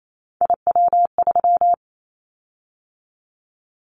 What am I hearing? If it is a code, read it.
Morse code, heard as a keyed tone, about 28 wpm, 700 Hz. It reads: IW3
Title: Callsign Prefixes for Top 50 DXCC - 28wpm